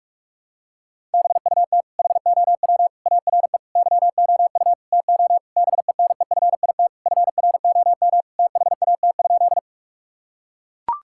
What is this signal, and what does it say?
Morse code, heard as a keyed tone, about 41 wpm, 700 Hz. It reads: BUT HOW ARE YOU TO BENEFIT FROM THAT?
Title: Sentences from Top 1000 Words - Review Part 6 - 45wpm